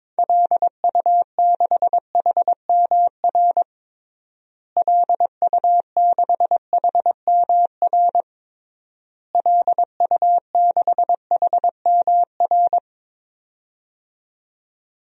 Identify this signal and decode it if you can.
Morse code, heard as a keyed tone, about 22 wpm, 700 Hz. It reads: LU6HMR LU6HMR LU6HMR
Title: DXCC-100 Call Signs - 6 Characters - Repeated 3x - 22wpm